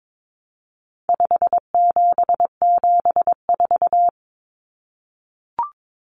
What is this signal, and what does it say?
Morse code, heard as a keyed tone, about 22 wpm, 700 Hz. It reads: 5774
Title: Common Cut Numbers - 4-Digit 22wpm